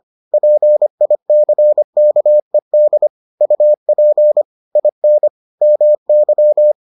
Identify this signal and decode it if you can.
Morse code, heard as a keyed tone, about 25 wpm, 600 Hz. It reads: PICKED UP IN MY